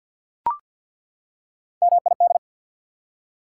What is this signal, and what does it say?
Morse code, heard as a keyed tone, about 48 wpm, 690 Hz. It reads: MID
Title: Prefixes and Suffixes 50wpm